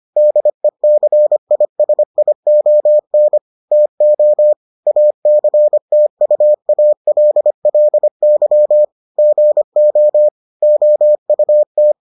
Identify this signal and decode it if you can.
Morse code, heard as a keyed tone, about 25 wpm, 600 Hz. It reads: DECISION TO ACTUALLY GO OUT